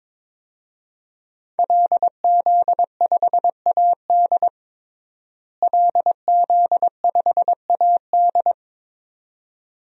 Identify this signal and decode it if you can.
Morse code, heard as a keyed tone, about 23 wpm, 700 Hz. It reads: LZ5AD LZ5AD
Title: DXCC-100 Call Signs - 5 Characters - Repeated 2x - 22wpm